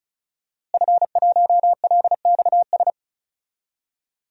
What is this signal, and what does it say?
Morse code, heard as a keyed tone, about 34 wpm, 700 Hz. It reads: F1LXS
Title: DXCC-100 Call Signs - 5 Characters - Repeated 3x - 35wpm